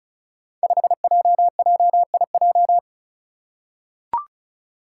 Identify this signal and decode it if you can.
Morse code, heard as a keyed tone, about 36 wpm, 700 Hz. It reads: 5JJIJ
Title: Five Challenging Letter-Number Combinations 35wpm